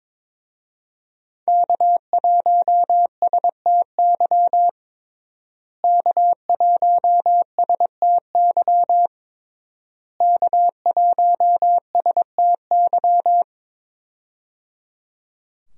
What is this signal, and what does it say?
Morse code, heard as a keyed tone, about 22 wpm, 700 Hz. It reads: K1STY K1STY K1STY
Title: USA Call Signs - 5 Characters - Repeated 3x - 22wpm